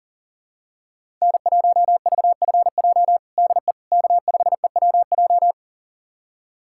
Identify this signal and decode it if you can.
Morse code, heard as a keyed tone, about 40 wpm, 700 Hz. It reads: N1VFJ DE K5EWJ
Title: USA Callsign DE Callsign - 5 Characters - 40wpm